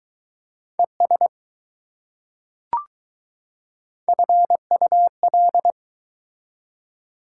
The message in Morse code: ES FUL

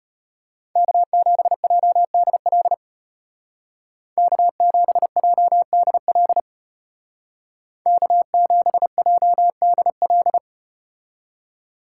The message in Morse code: K7JDL K7JDL K7JDL

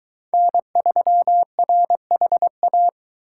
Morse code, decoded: N3RHA

